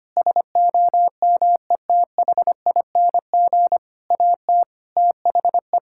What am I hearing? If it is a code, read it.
Morse code, heard as a keyed tone, about 25 wpm, 700 Hz. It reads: SOMETHING AT THE
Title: Sentences from Top 1000 Words - Review Part 6 - 25wpm